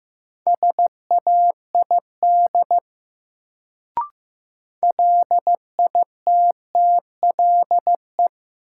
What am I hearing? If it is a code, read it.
Morse code, heard as a keyed tone, about 15 wpm, 700 Hz. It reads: SAID LITTLE
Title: Sets of 2 Words - New Word Drill - 15wpm